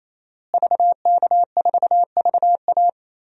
Morse code, heard as a keyed tone, about 28 wpm, 700 Hz. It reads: VK4VA
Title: DXCC-100 Call Signs - 5 Characters - Repeated 3x - 28wpm